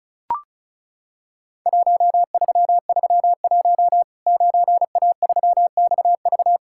Morse code, heard as a keyed tone, about 36 wpm, 700 Hz. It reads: 1331 9A3XV
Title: Worked All Europe DX Contest - QTC Exchange - 35wpm